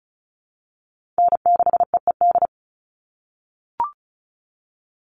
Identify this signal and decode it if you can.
Morse code, heard as a keyed tone, about 34 wpm, 700 Hz. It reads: N6EED
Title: Call Signs - 35wpm